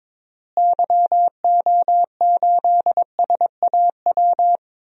Morse code, heard as a keyed tone, about 22 wpm, 700 Hz. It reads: YO8SAW
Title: DXCC-100 Call Signs - 6 Characters - Repeated 3x - 22wpm